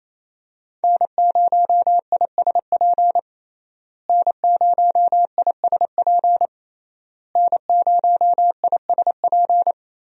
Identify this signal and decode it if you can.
Morse code, heard as a keyed tone, about 28 wpm, 700 Hz. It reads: N0ISP N0ISP N0ISP
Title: USA Call Signs - 5 Characters - Repeated 3x - 28wpm